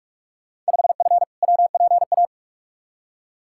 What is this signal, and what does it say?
Morse code, heard as a keyed tone, about 42 wpm, 700 Hz. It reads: 5F WPA
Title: Field Day Exchanges Encore - 45wpm